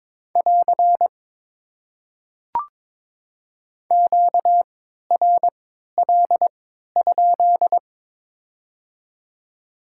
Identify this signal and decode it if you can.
Morse code, heard as a keyed tone, about 22 wpm, spaced as for 12 wpm, 700 Hz. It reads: <AR> QRL?
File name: Top 90 QSO Elements - 3x Character Spacing - 22wpm